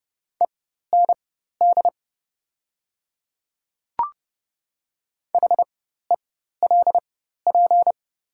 Morse code, heard as a keyed tone, about 30 wpm, spaced as for 14 wpm, 700 Hz. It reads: END HELP
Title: 4x Character Spacing - Top 200 Words - 30wpm